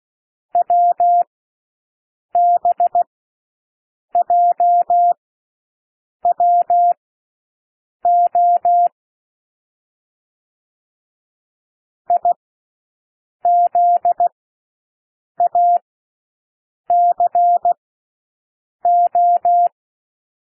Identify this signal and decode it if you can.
Morse code, heard as a keyed tone, about 16 wpm, spaced as for 6 wpm, 700 Hz. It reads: WBJWO IZACO
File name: cwmp3 / L1 SOUND LETTERS 16 wpm